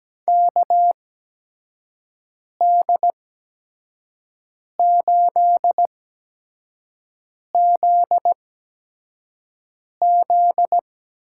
Morse code, heard as a keyed tone, about 17 wpm, spaced as for 5 wpm, 700 Hz. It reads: KD8ZZ
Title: USA Call Signs - 6 Characters - 8x Character Spacing - 17wpm